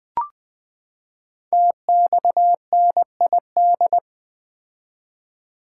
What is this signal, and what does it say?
Morse code, heard as a keyed tone, about 20 wpm, 700 Hz. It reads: TXNID
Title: Code Groups - 5 Letter:Numbers - 20wpm